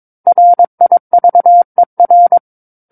Morse code, heard as a keyed tone, about 22 wpm, 700 Hz. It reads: RIVER
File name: Book One / 13-Chapter 13 - 22 WPM